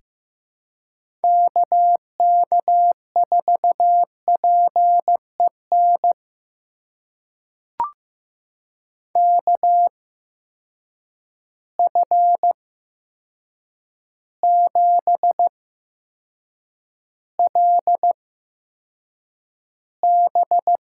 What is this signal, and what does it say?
Morse code, heard as a keyed tone, about 15 wpm, spaced as for 7 wpm, 700 Hz. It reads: KK4PEN KF7LB
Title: USA Call Signs - 6 Characters - 8x Character Spacing - 15wpm